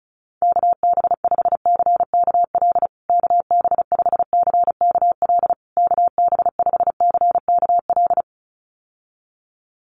Morse code, heard as a keyed tone, about 34 wpm, 700 Hz. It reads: KB5CKL KB5CKL KB5CKL
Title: USA Call Signs Repeated 3x 35wpm